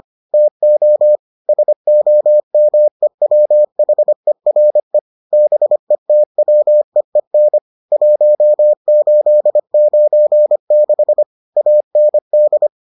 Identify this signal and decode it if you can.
Morse code, heard as a keyed tone, about 25 wpm, 600 Hz. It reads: TO SOMEWHERE BETWEEN 1896 AND